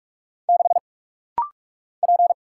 Morse code, heard as a keyed tone, about 46 wpm, 700 Hz. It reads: 6 P
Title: Single Letter-Number Alt - Rapid-Fire 45wpm